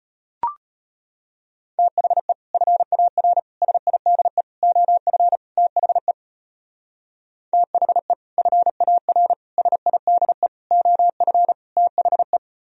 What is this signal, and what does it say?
Morse code, heard as a keyed tone, about 32 wpm, 700 Hz. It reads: THE FAR SIDE OF THE THE FAR SIDE OF THE